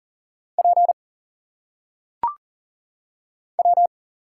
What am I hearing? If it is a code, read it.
Morse code, heard as a keyed tone, about 39 wpm, 700 Hz. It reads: P W
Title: Single Letters - Phonetic Pronunciation 40wpm